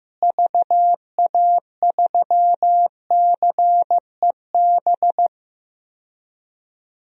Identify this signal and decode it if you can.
Morse code, heard as a keyed tone, about 15 wpm, 700 Hz. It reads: VA3CEB
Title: DXCC-100 Call Signs - 6 Characters - Repeated 2x - 15wpm